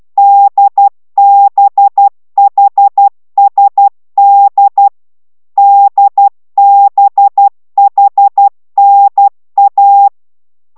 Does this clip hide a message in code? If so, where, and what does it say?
Yes: Morse code from 0.2 to 10.1 s, heard as a keyed tone, about 12 wpm, 800 Hz. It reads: DBHSD DBHNA